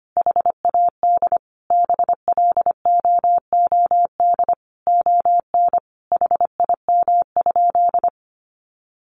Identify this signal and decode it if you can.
Morse code, heard as a keyed tone, about 25 wpm, 700 Hz. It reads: HAD BLOOD ON HIM?
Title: Sentences from Top 1100 Words 25wpm